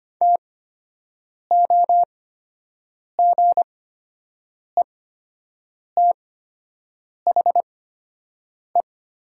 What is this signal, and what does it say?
Morse code, heard as a keyed tone, about 25 wpm, spaced as for 7 wpm, 700 Hz. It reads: TOGETHE
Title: 8x Character Spacing - Top 300 Words - 25wpm